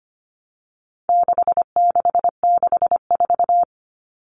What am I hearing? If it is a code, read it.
Morse code, heard as a keyed tone, about 25 wpm, 700 Hz. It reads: 6664
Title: Numbers - 4-Digit 25wpm